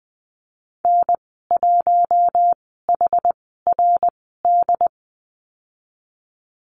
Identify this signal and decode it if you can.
Morse code, heard as a keyed tone, about 20 wpm, spaced as for 15 wpm, 700 Hz. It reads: N1HRD
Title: 069 Lesson 12g - Letter d - Call Signs 20wpm